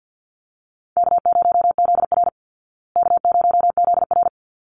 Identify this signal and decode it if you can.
Morse code, heard as a keyed tone, about 50 wpm, 700 Hz. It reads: K0ZR K0ZR
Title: Call Signs Repeated 3x - 50wpm